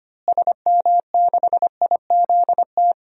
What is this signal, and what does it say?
Morse code, heard as a keyed tone, about 25 wpm, 700 Hz. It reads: SM6IZT
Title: DXCC-100 Call Signs - 6 Characters - Repeated 3x - 25wpm